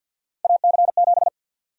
Morse code, heard as a keyed tone, about 51 wpm, 700 Hz. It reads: AC7
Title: USA Call Sign Prefixes - 50wpm